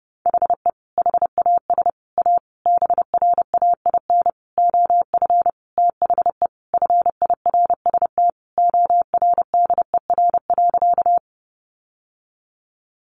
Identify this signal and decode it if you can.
Morse code, heard as a keyed tone, about 30 wpm, 700 Hz. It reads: HE HAS A BRAIN OF THE FIRST ORDER.
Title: Sentences from Top 1100 Words 30wpm